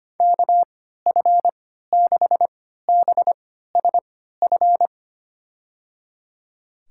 Morse code, heard as a keyed tone, about 25 wpm, spaced as for 14 wpm, 700 Hz. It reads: KF6BSF
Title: USA Call Signs - 6 Characters - 3x Character Spacing - 25wpm